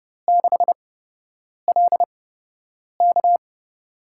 Morse code, heard as a keyed tone, about 32 wpm, spaced as for 8 wpm, 700 Hz. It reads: 6LK
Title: USA Call Signs - 5 Characters - 8x Character Spacing - 30wpm